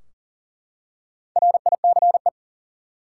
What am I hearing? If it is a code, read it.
Morse code, heard as a keyed tone, about 40 wpm, 700 Hz. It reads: RICE
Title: Four Letter Words - 40wpm